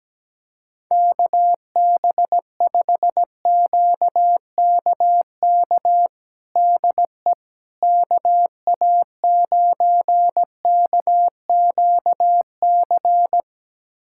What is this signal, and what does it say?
Morse code, heard as a keyed tone, about 17 wpm, 700 Hz. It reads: KB5QKK DE KA9KQC
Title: USA Callsign DE Callsign - 6 Characters - 17wpm